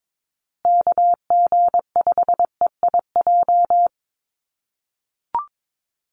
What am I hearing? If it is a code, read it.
Morse code, heard as a keyed tone, about 22 wpm, 700 Hz. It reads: KG5EIJ
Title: USA Call Signs Repeated 3x 22wpm